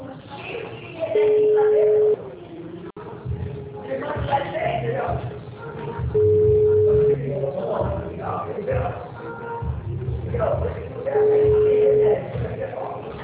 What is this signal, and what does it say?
Ringback tone (Europe, 425 Hz), with speech.
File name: out- -2005- - - .26256